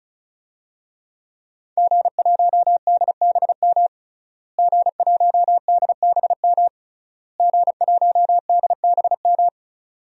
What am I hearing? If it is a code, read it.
Morse code, heard as a keyed tone, about 36 wpm, 700 Hz. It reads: G1DBM G1DBM G1DBM
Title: DXCC-100 Call Signs - 5 Characters - Repeated 3x - 35wpm